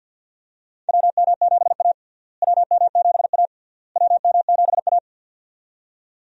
Morse code, heard as a keyed tone, about 47 wpm, 700 Hz. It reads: WM7A WM7A WM7A
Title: USA Call Signs - 4 Characters - Repeated 3x - 50wpm